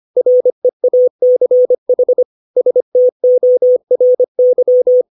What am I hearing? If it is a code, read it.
Morse code, heard as a keyed tone, about 25 wpm, 500 Hz. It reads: REACH STORY
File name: 632 words / 632 Words 25WPM